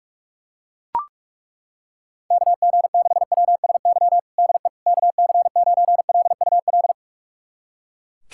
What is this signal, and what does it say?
Morse code, heard as a keyed tone, about 42 wpm, 700 Hz. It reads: KG6WSY DE KC9LUL
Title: USA Callsign DE Callsign - 6 Characters - 45wpm